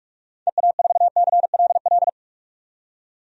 Morse code, heard as a keyed tone, about 43 wpm, 700 Hz. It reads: EA4CLL
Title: DXCC-100 Call Signs - 6 Characters - 45wpm